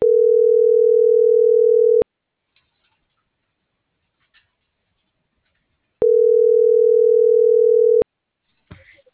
Ringback tone (North America), 440 + 480 Hz.